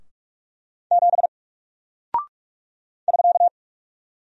Morse code, heard as a keyed tone, about 45 wpm, 700 Hz. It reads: 7 <SK>
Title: Single Character (Letter:Number:Punctuation:Prosign) Alt 45wpm